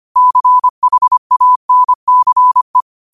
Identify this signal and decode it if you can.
Morse code, heard as a keyed tone, about 25 wpm, 1000 Hz. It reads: CHANCE
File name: qotdfiles / 25wpm